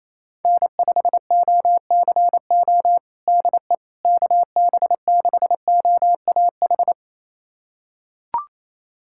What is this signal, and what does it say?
Morse code, heard as a keyed tone, about 28 wpm, 700 Hz. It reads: N5OCO DE KB6OAH
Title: USA Callsign DE Callsign 28wpm